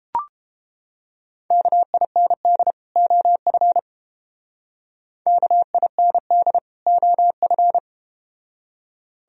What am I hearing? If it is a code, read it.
Morse code, heard as a keyed tone, about 32 wpm, 700 Hz. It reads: KIND OF KIND OF